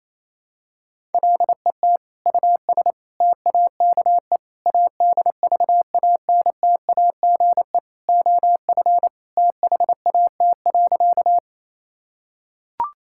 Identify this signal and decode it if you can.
Morse code, heard as a keyed tone, about 28 wpm, 700 Hz. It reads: LET US TAKE ADVANTAGE OF THAT.
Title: Sentences from Top 1000 Words - Review Part 6 - 28wpm